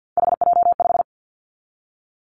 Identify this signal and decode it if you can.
Morse code, heard as a keyed tone, about 50 wpm, 690 Hz. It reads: HJ5
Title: Callsign Prefixes for Top 100 DXCC - 50wpm